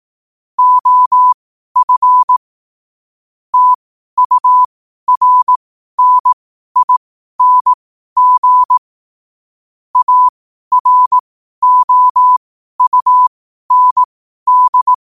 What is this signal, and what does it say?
Morse code, heard as a keyed tone, about 18 wpm, spaced as for 12 wpm, 1000 Hz. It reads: OF TURNING AROUND
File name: qotdfiles / 12wpm